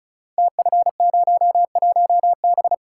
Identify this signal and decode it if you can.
Morse code, heard as a keyed tone, about 36 wpm, 690 Hz. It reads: TF01B
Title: Code Groups - 5 Letter:Numbers - 35wpm